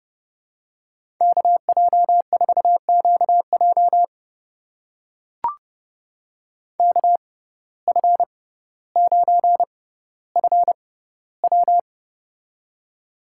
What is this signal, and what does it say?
Morse code, heard as a keyed tone, about 29 wpm, spaced as for 15 wpm, 700 Hz. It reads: KJ4QJ KF9FW
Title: USA Call Signs - 5 Characters - 6x Character Spacing - 30wpm